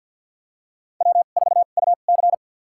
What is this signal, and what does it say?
Morse code, heard as a keyed tone, about 53 wpm, spaced as for 37 wpm, 700 Hz. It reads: W4UC
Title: USA Call Signs - 4 Characters - 2x Character Spacing - 50wpm